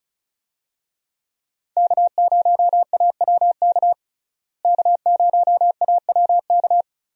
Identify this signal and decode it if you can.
Morse code, heard as a keyed tone, about 36 wpm, 700 Hz. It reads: K0AWK K0AWK
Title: USA Call Signs - 5 Characters - Repeated 3x - 35wpm